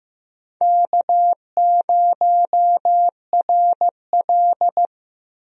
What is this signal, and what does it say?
Morse code, heard as a keyed tone, about 15 wpm, 700 Hz. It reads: K0RL
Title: USA Call Signs - 4 Characters - Repeated 2x - 15wpm